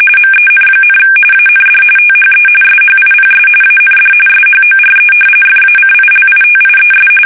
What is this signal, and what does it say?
RTTY, about 45.45 baud. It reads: MEN, AND THE SHIPS AND SHORE FACILITIES
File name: fsk850 / usnavy3